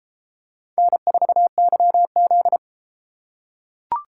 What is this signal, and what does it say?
Morse code, heard as a keyed tone, about 28 wpm, 700 Hz. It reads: N4YZ